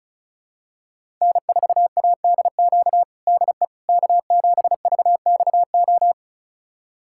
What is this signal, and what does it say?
Morse code, heard as a keyed tone, about 34 wpm, 700 Hz. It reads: N4ADQ DE K7VXO
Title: USA Callsign DE Callsign - 5 Characters - 35wpm